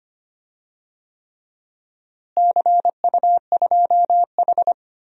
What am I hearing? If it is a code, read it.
Morse code, heard as a keyed tone, about 25 wpm, 700 Hz. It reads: CU2H
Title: DXCC-100 Call Signs - 4 Characters - Repeated 3x - 25wpm